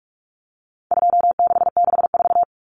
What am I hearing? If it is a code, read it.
Morse code, heard as a keyed tone, about 44 wpm, 700 Hz. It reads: 2664
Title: Common Cut Numbers - 4-Digit 45wpm